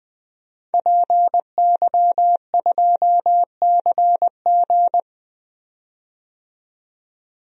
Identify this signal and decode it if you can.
Morse code, heard as a keyed tone, about 20 wpm, 700 Hz. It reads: PY2CG
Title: DXCC-100 Call Signs - 5 Characters - 20wpm